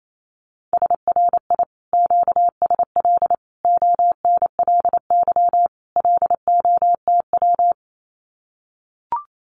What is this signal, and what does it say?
Morse code, heard as a keyed tone, about 28 wpm, 700 Hz. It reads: SRI QSL ONLY LOTW
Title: Common QSO Phrases Encore 28wpm